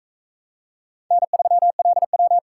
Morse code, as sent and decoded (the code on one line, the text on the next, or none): -. ...-- .-.. .--
N3LW